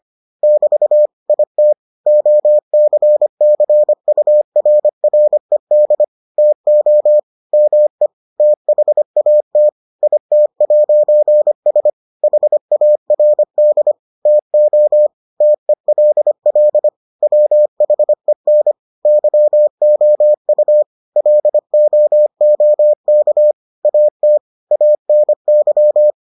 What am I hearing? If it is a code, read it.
Morse code, heard as a keyed tone, about 25 wpm, 600 Hz. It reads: <BT> IT OCCURRED TO ME THAT IT'S HARD TO TELL WHEN YOU LOOK AT ANY